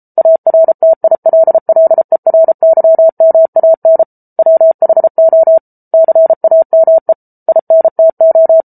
Morse code, heard as a keyed tone, about 34 wpm, 650 Hz. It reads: ARTILLERYMAN WHO CAME INTO